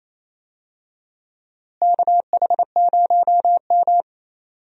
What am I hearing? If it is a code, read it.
Morse code, heard as a keyed tone, about 28 wpm, 700 Hz. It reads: KH0M
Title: USA Call Signs - 4 Characters - Repeated 2x - 28wpm